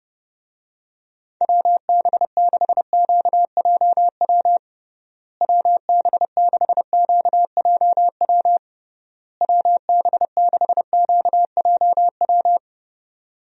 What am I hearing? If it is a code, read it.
Morse code, heard as a keyed tone, about 30 wpm, 700 Hz. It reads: WB6QJW WB6QJW WB6QJW
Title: USA Call Signs - 6 Characters - Repeated 3x - 30wpm